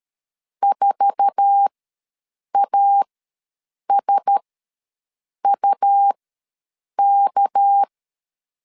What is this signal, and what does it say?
Morse code, heard as a keyed tone, about 13 wpm, spaced as for 7 wpm, 800 Hz. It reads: 4ASUK
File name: LEZIONI / 8 YJ45 / 8 YJ4 ws 12wpm 3249